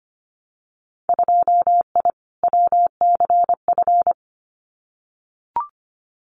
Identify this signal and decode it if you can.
Morse code, heard as a keyed tone, about 25 wpm, 700 Hz. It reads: 2I WCF
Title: Winter Field Day Exchanges 25wpm